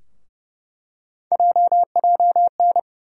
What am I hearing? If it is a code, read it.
Morse code, heard as a keyed tone, about 30 wpm, 700 Hz. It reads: JJN
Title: USA Call Sign Postfixes - 3 Letters 30wpm